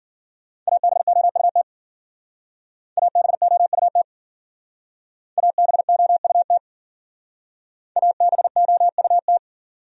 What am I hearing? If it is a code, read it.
Morse code, heard as a keyed tone, about 52 wpm, 700 Hz. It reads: ABOUT ABOUT ABOUT ABOUT